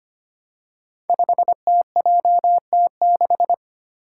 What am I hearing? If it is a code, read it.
Morse code, heard as a keyed tone, about 25 wpm, 700 Hz. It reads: 5TJT6
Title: Five Challenging Letter-Number Combinations 25wpm